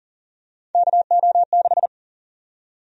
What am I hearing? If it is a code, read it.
Morse code, heard as a keyed tone, about 40 wpm, 700 Hz. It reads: KO6
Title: USA Call Sign Prefixes - 40wpm